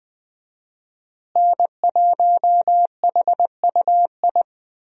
Morse code, heard as a keyed tone, about 20 wpm, 700 Hz. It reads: N1HUI
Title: USA Call Signs - 5 Characters - Repeated 3x - 20wpm